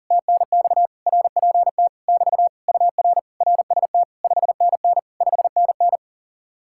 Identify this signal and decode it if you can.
Morse code, heard as a keyed tone, about 53 wpm, spaced as for 47 wpm, 700 Hz. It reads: TNX RPT <BT> UR RST 5NN 5NN
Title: Speed-Racing Common QSO Phrases Encore - 40wpm